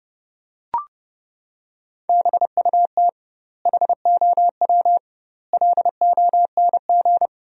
Morse code, heard as a keyed tone, about 28 wpm, 700 Hz. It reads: BUT HOW LONG
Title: 2x Word-Spacing - Sentences from Top 500 Words - 30wpm